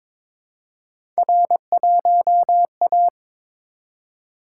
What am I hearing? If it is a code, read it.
Morse code, heard as a keyed tone, about 23 wpm, 700 Hz. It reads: R1A
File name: practice / 22wpm / Three Letter-Number Combinations - 22wpm